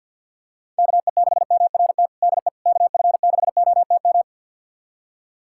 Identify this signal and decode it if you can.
Morse code, heard as a keyed tone, about 51 wpm, 700 Hz. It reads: KE6MRT DE KF6OTM
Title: USA Callsign DE Callsign - 6 Characters - 50wpm